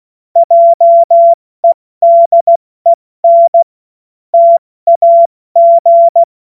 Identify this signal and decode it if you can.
Morse code, heard as a keyed tone, about 16 wpm, 680 Hz. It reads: JEDEN TAG
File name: Morsekurs / Witze / Witz 20. September 2022 70 F80